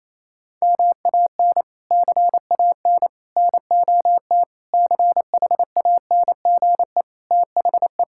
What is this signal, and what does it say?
Morse code, heard as a keyed tone, about 28 wpm, 700 Hz. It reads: MAN CAN NOT CHANGE THE
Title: Sentences from Top 1000 Words - Review Part 5 - 28wpm